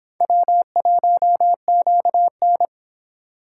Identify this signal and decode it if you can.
Morse code, heard as a keyed tone, about 26 wpm, 700 Hz. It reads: W1QN